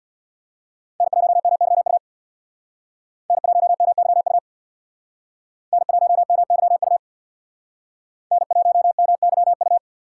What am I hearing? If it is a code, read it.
Morse code, heard as a keyed tone, about 62 wpm, 700 Hz. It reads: N1MYU N1MYU N1MYU N1MYU
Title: Speed Racing - USA 1x3 Call Signs 50wpm